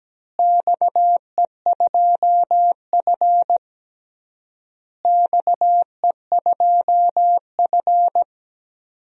Morse code, heard as a keyed tone, about 17 wpm, 700 Hz. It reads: XE2F XE2F
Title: DXCC-100 Call Signs - 4 Characters - Repeated 3x - 17wpm